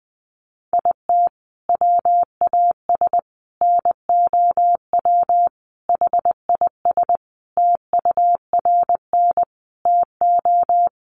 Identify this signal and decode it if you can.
Morse code, heard as a keyed tone, about 20 wpm, 700 Hz. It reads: IT WAS NOW HIS TURN TO
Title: Sentences from Top 200 Words 20wpm